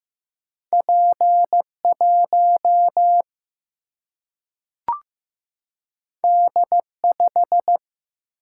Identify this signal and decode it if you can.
Morse code, heard as a keyed tone, about 15 wpm, 700 Hz. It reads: P1 D5
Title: Single-Letter Single-Number 15wpm